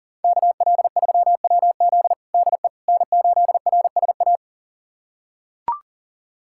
Morse code, heard as a keyed tone, about 40 wpm, 700 Hz. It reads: KL3WZ DE N8RSA
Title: USA Callsign DE Callsign - 5 Characters - 40wpm